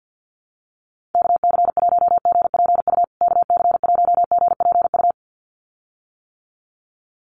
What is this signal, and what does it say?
Morse code, heard as a keyed tone, about 50 wpm, 700 Hz. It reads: KC1GPU KC1GPU
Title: USA Call Signs Repeated 2x 50wpm